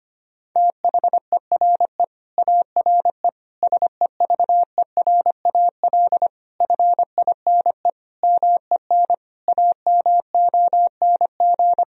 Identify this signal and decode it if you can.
Morse code, heard as a keyed tone, about 25 wpm, 700 Hz. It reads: THERE ARE SEVERAL FINE MEN AMONG
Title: Sentences from Top 1000 Words - Review Part 2 - 25wpm